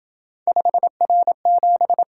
Morse code, heard as a keyed tone, about 27 wpm, 700 Hz. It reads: 5R7